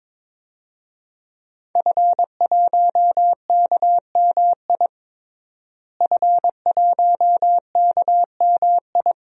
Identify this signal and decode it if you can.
Morse code, heard as a keyed tone, about 22 wpm, 700 Hz. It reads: F1KMI F1KMI
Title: DXCC-100 Call Signs - 5 Characters - Repeated 2x - 22wpm